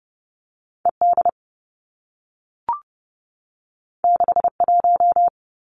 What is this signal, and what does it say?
Morse code, heard as a keyed tone, about 30 wpm, 700 Hz. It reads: E D 6 1
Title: Two Challenging Letter-Number Combinations 30wpm